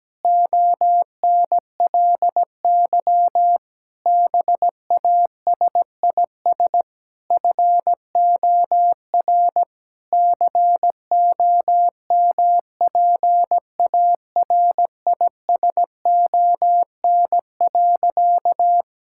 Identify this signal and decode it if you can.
Morse code, heard as a keyed tone, about 17 wpm, 700 Hz. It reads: ONLY BASIS FOR COMPARISON.
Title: Sentences from Top 1600 Words 17wpm